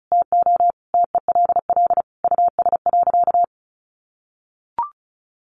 Morse code, heard as a keyed tone, about 34 wpm, 700 Hz. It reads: TO TELL US.
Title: Sentences from Top 1400 Words 35wpm